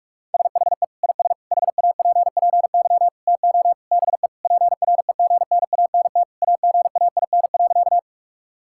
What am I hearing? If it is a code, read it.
Morse code, heard as a keyed tone, about 46 wpm, 700 Hz. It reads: SHE IS HAPPY TO BE PREGNANT AGAIN.
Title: Sentences from Top 1000 Words - Review Part 6 - 45wpm